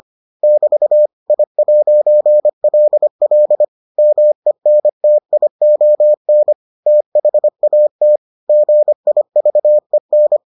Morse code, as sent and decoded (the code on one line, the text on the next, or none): -...- / .. .----. .-.. .-.. / -- . -. - .. --- -. / - .... .- - / --. .. ...- . -.
<BT> I'LL MENTION THAT GIVEN